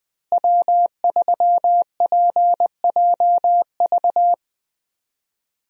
Morse code, heard as a keyed tone, about 21 wpm, 700 Hz. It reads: W3PJV
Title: USA Call Signs - 5 Characters - Repeated 3x - 20wpm